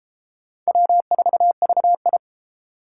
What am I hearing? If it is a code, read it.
Morse code, heard as a keyed tone, about 32 wpm, 700 Hz. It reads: W4VI